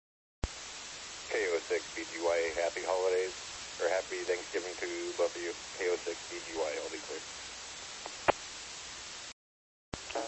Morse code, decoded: none

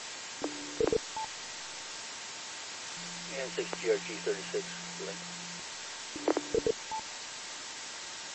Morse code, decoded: none

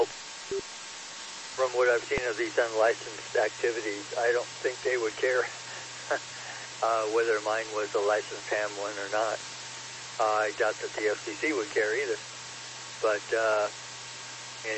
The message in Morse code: none

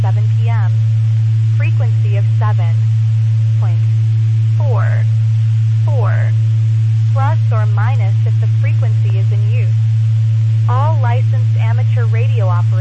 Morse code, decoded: none

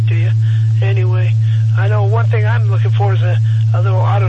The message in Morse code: none

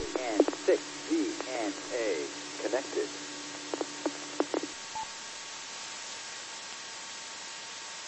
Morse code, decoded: none

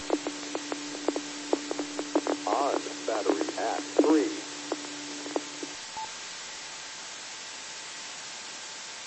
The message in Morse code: none

